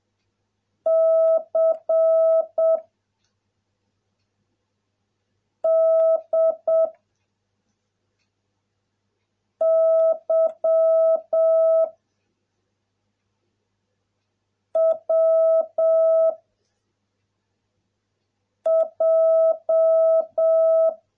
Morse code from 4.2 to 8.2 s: D